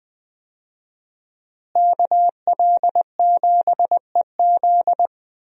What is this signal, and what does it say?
Morse code, heard as a keyed tone, about 20 wpm, 700 Hz. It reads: KL7EZ
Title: DXCC-100 Call Signs - 5 Characters - Repeated 3x - 20wpm